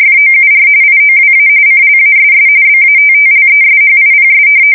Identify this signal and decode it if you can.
RTTY, about 45.45 baud. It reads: FOUR IOWA CLASS BATTLESHIPS
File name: fsk170 / battleshipnj